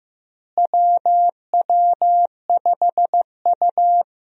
Morse code, heard as a keyed tone, about 15 wpm, 700 Hz. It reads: WW5U
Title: USA Call Signs - 4 Characters - Repeated 2x - 15wpm